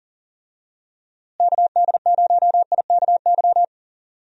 Morse code, heard as a keyed tone, about 37 wpm, 700 Hz. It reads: KD0IKY
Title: USA Call Signs - 6 Characters - Repeated 3x - 40wpm